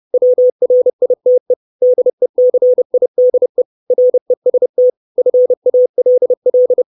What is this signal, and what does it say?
Morse code, heard as a keyed tone, about 30 wpm, 500 Hz. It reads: WRITE DECIDE REST FALL
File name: 632 Words 30WPM